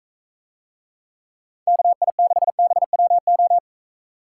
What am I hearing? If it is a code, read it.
Morse code, heard as a keyed tone, about 42 wpm, 700 Hz. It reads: KI6BWO